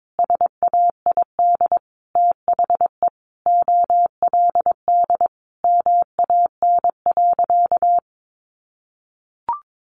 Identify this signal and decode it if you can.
Morse code, heard as a keyed tone, about 22 wpm, 700 Hz. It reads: SAID THE OLD MAN.
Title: Sentences from Top 500 Words 22wpm